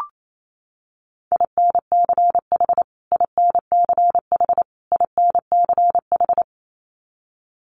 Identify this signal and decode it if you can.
Morse code, heard as a keyed tone, about 28 wpm, 700 Hz. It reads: INCH INCH INCH
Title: Top 500 Words Repeated 28wpm